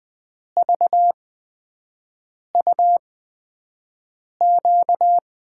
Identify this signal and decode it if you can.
Morse code, heard as a keyed tone, about 21 wpm, spaced as for 5 wpm, 700 Hz. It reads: VUQ
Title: USA Call Signs - 5 Characters - 8x Character Spacing - 20wpm